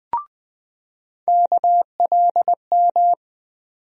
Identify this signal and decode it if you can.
Morse code, heard as a keyed tone, about 20 wpm, 700 Hz. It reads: KLM
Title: USA Call Sign Postfixes - 3 Letters 20wpm